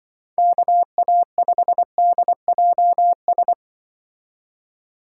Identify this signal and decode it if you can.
Morse code, heard as a keyed tone, about 25 wpm, 700 Hz. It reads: KA5DJS
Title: Speed Racing - USA 2x3 Call Signs 20wpm